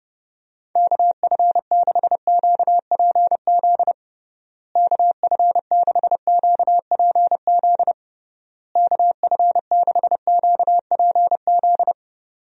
Morse code, heard as a keyed tone, about 30 wpm, 700 Hz. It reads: KF6QPZ KF6QPZ KF6QPZ
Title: USA Call Signs - 6 Characters - Repeated 3x - 30wpm